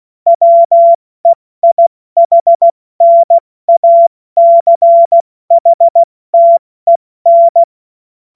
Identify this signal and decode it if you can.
Morse code, heard as a keyed tone, about 16 wpm, 680 Hz. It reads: WEIHNACHTEN
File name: Morsekurs / Witze / Witz 10. Januar 2023 70 F80